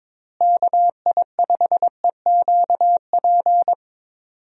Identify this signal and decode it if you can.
Morse code, heard as a keyed tone, about 21 wpm, 700 Hz. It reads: KI5EQP
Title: USA Call Signs - 6 Characters - Repeated 3x - 22wpm